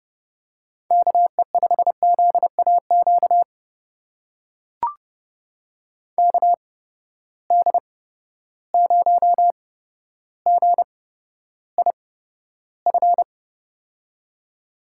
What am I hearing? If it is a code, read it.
Morse code, heard as a keyed tone, about 31 wpm, spaced as for 13 wpm, 700 Hz. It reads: KE5ZAQ KD0GIF
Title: USA Call Signs - 6 Characters - 8x Character Spacing - 30wpm